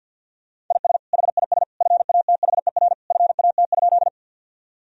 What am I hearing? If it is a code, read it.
Morse code, heard as a keyed tone, about 50 wpm, 700 Hz. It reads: IS HIS FATHER FAT?
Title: Sentences from Top 1000 Words - Review Part 4 - 50wpm